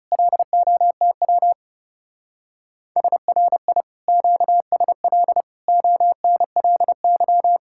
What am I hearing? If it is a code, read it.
Morse code, heard as a keyed tone, about 36 wpm, 700 Hz. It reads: LOTW SRI QSL ONLY
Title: Speed-Racing Common QSO Phrases Encore - 30wpm